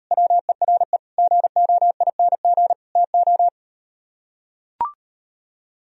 Morse code, heard as a keyed tone, about 38 wpm, 700 Hz. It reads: WERE GOING TO